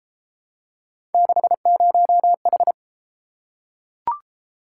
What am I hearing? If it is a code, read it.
Morse code, heard as a keyed tone, about 28 wpm, 700 Hz. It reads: 60H